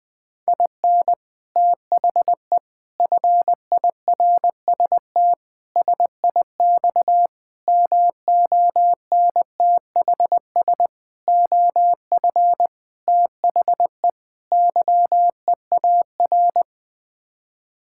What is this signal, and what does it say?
Morse code, heard as a keyed tone, about 21 wpm, 700 Hz. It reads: IN THE FIRST SIX MONTHS OF THE YEAR
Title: Sets of 8 Words - 20wpm